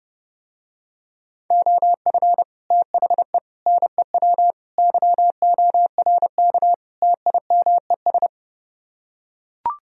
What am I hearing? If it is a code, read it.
Morse code, heard as a keyed tone, about 30 wpm, 700 Hz. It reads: OF THE NEW YORK TIMES